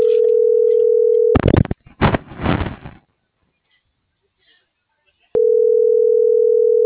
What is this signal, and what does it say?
Ringback tone (North America), 440 + 480 Hz.